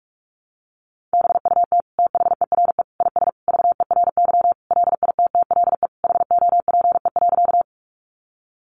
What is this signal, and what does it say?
Morse code, heard as a keyed tone, about 44 wpm, 700 Hz. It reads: BUT THERE IS VERY LITTLE HOPE.
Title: Sentences from Top 1000 Words 45wpm